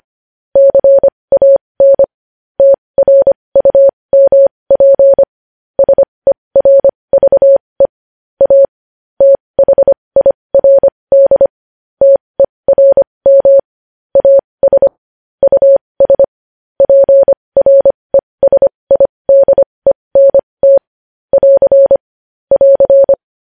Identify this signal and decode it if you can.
Morse code, heard as a keyed tone, about 25 wpm, spaced as for 20 wpm, 550 Hz. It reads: CAN TRUMP SERVE A THIRD TERM AS US PRESIDENT <AR> <AR>
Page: News Headlines in Morse Code at 20 WPM